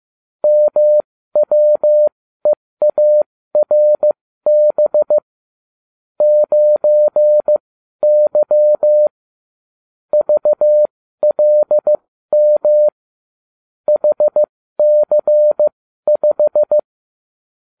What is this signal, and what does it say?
Morse code, heard as a keyed tone, about 15 wpm, spaced as for 12 wpm, 600 Hz. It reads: MWEARB 9Y VLM HC5